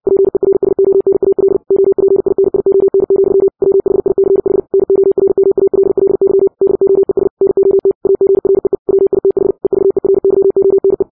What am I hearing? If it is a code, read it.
Morse code, heard as a keyed tone, about 60 wpm, 390 Hz. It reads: PERSONAL OPINION, WHICH NORMALLY DOES NOT AGREE WITH EVERYONE
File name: 60wpm